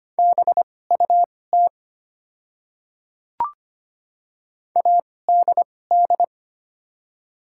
Morse code, heard as a keyed tone, about 25 wpm, spaced as for 18 wpm, 690 Hz. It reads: BUT ADD
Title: 2x Character Spacing - Top 200 Words - 25wpm